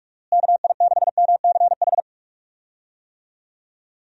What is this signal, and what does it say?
Morse code, heard as a keyed tone, about 46 wpm, 700 Hz. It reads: KI6MCH
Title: USA Call Signs - 6 Characters - Repeated 2x - 45wpm